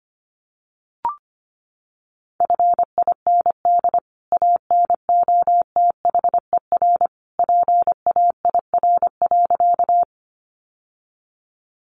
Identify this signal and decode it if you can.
Morse code, heard as a keyed tone, about 25 wpm, 700 Hz. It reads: FIND ANOTHER PAIR.
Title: Sentences from Top 1200 Words 25wpm